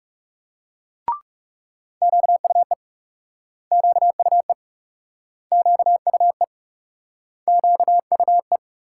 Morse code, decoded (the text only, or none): QUE QUE QUE QUE